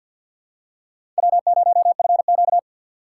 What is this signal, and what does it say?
Morse code, heard as a keyed tone, about 49 wpm, 690 Hz. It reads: W0FQ
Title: USA Call Signs - 2x Character Spacing - 50wpm